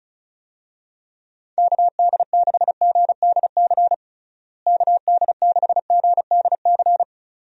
Morse code, heard as a keyed tone, about 36 wpm, 700 Hz. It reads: KD6GDC KD6GDC
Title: USA Call Signs - 6 Characters - Repeated 2x - 35wpm